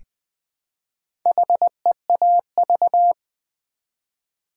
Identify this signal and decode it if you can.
Morse code, heard as a keyed tone, about 20 wpm, 700 Hz. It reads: HEAV
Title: Build A Word - 1 Letter At A Time - 3 to 5-Letter Words - 20wpm